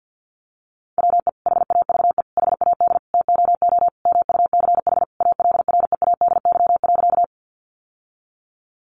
Morse code, heard as a keyed tone, about 50 wpm, 700 Hz. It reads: WE HAVE HAD TOO MUCH ALREADY.
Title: Sentences from Top 700 Words 50wpm